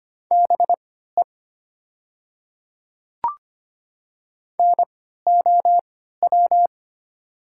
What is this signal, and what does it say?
Morse code, heard as a keyed tone, about 25 wpm, spaced as for 14 wpm, 700 Hz. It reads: BE NOW
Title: 3x Character Spacing - Top 100 Words - 25wpm